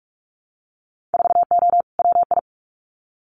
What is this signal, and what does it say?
Morse code, heard as a keyed tone, about 46 wpm, 700 Hz. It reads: 4O WI
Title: Winter Field Day Exchanges 45wpm